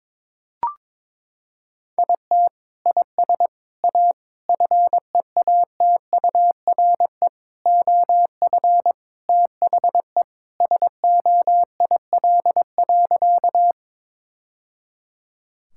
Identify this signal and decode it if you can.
Morse code, heard as a keyed tone, about 22 wpm, 700 Hz. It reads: IT IS A FEATURE OF THE SOIL.
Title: Sentences from Top 1000 Words - Review Part 5 - 22wpm